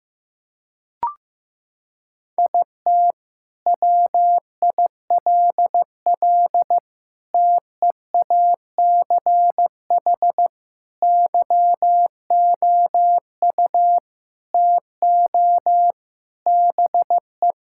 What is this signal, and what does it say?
Morse code, heard as a keyed tone, about 14 wpm, 700 Hz. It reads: IT WILL TEACH YOU TO BE
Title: Sentences from Top 1600 Words 15wpm